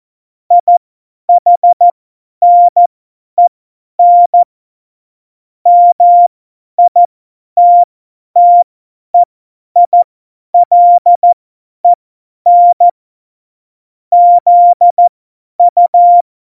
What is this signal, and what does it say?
Morse code, heard as a keyed tone, about 14 wpm, spaced as for 10 wpm, 700 Hz. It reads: IHNEN MITTEILEN ZU
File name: Morsekurs / Witze / Witz 06. Oktober 2020, 50 F70